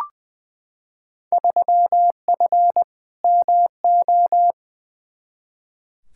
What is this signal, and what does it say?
Morse code, heard as a keyed tone, about 20 wpm, 700 Hz. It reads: 3F MO
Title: Field Day Exchanges Encore - 20wpm